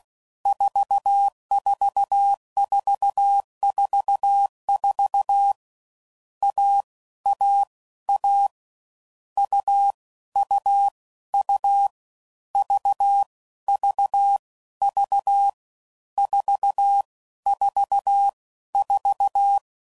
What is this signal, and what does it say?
Morse code, heard as a keyed tone, about 16 wpm, 800 Hz. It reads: 44444 A A A U U U V V V 4 4 4
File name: Combinations / Difficult Combos / AUV AUV4